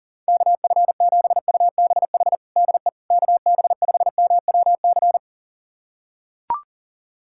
Morse code, heard as a keyed tone, about 40 wpm, 700 Hz. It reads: KF7UBH DE KB5MWC
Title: USA Callsign DE Callsign - 6 Characters - 40wpm